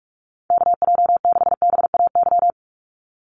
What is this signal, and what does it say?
Morse code, heard as a keyed tone, about 46 wpm, 690 Hz. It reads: KJ6BAY
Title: USA Call Signs 45wpm